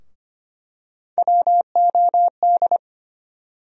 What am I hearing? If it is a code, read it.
Morse code, heard as a keyed tone, about 25 wpm, 700 Hz. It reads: WOD
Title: USA Call Sign Postfixes - 3 Letters 25wpm